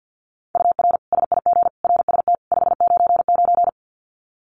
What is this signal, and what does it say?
Morse code, heard as a keyed tone, about 50 wpm, 700 Hz. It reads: UR SIG RST 599
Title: Common QSO Phrases 50wpm